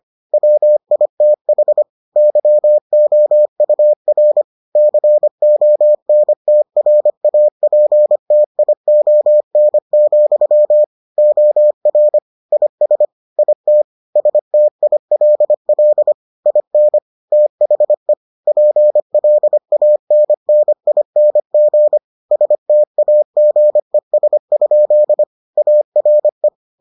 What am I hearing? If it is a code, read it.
Morse code, heard as a keyed tone, about 25 wpm, 600 Hz. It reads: WITH YOUR CONTRAPTION, OR IS IT STILL IN THE PLANNING STAGES? ARE